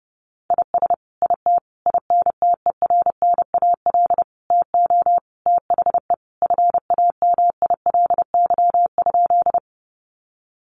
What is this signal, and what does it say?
Morse code, heard as a keyed tone, about 28 wpm, 700 Hz. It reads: IS IT INTERNAL TO THE FAMILY?
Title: Sentences from Top 1400 Words 30wpm